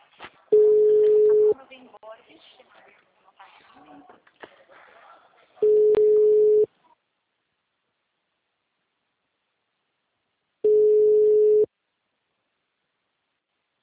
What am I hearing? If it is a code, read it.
Ringback tone (Europe, 425 Hz), with speech.
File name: out- -2003- - - .42318